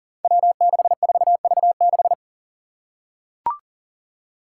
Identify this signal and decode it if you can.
Morse code, heard as a keyed tone, about 39 wpm, 700 Hz. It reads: W64V6
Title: Five Challenging Letter-Number Combinations 40wpm